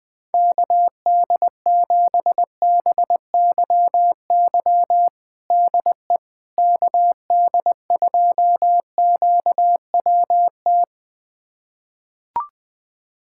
Morse code, heard as a keyed tone, about 21 wpm, 700 Hz. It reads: KD7BYY DE KD2QWT
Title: USA Callsign DE Callsign - 6 Characters - 20wpm